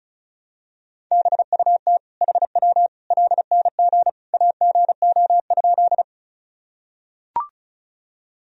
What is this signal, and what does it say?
Morse code, heard as a keyed tone, about 36 wpm, 700 Hz. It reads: BUT HW LNG AGO?
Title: Sentences from Top 500 Words and Abbreviations 35wpm